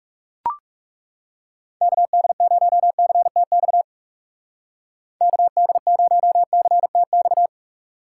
Morse code, heard as a keyed tone, about 38 wpm, 700 Hz. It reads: KD0CTX KD0CTX